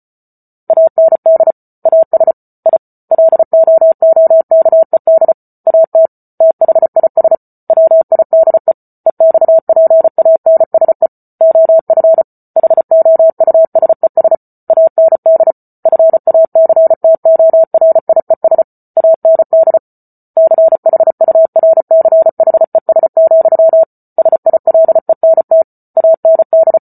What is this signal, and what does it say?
Morse code, heard as a keyed tone, about 34 wpm, 650 Hz. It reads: AND AS I LOOKED AT THIS WIDE EXPANSE OF HOUSES AND FACTORIES AND CHURCHES, SILENT AND